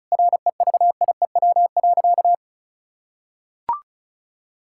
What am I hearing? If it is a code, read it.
Morse code, heard as a keyed tone, about 36 wpm, 700 Hz. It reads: REVIEW.
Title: Sentences from Top 1000 Words - Review Part 6 - 35wpm